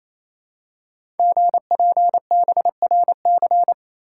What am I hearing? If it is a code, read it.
Morse code, heard as a keyed tone, about 28 wpm, 700 Hz. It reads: GPBRC
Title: Code Groups - 5 Letters - 28wpm